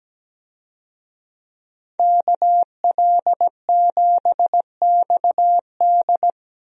Morse code, heard as a keyed tone, about 17 wpm, 700 Hz. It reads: KL7XD
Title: DXCC-100 Call Signs - 5 Characters - 17wpm